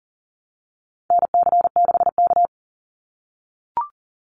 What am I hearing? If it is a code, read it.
Morse code, heard as a keyed tone, about 40 wpm, 700 Hz. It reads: NC6K
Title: Call Signs - 40wpm